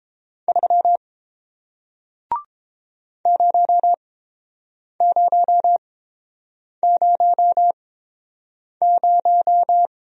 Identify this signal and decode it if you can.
Morse code, heard as a keyed tone, about 30 wpm, spaced as for 7 wpm, 700 Hz. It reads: 3 0000